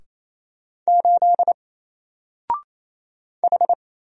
Morse code, heard as a keyed tone, about 27 wpm, 700 Hz. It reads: 8 H
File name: Single-Letter-Number-28wpm